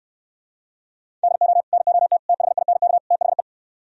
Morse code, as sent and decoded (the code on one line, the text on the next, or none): -.. --- / -. --- - / - .... .. -. -.- / - .... .
DO NOT THINK THE